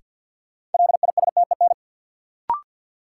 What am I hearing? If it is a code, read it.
Morse code, heard as a keyed tone, about 53 wpm, 700 Hz. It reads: LISTEN
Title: Sentences from Top 400 Words - New Word Drill - 50wpm